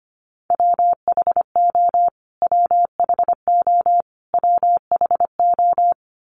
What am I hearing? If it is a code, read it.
Morse code, heard as a keyed tone, about 25 wpm, 700 Hz. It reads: WHO WHO WHO
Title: Top 200 Words Repeated 25wpm